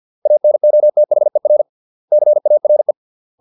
Morse code, heard as a keyed tone, about 50 wpm, 600 Hz. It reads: ANOTHER CARE